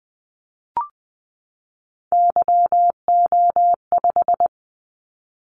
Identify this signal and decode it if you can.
Morse code, heard as a keyed tone, about 20 wpm, 700 Hz. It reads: YO5
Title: Callsign Prefixes for Top 25 DXCC - 20wpm